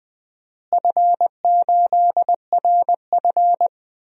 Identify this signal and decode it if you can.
Morse code, heard as a keyed tone, about 21 wpm, 700 Hz. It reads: F8RF
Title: DXCC-100 Call Signs - 4 Characters - Repeated 3x - 20wpm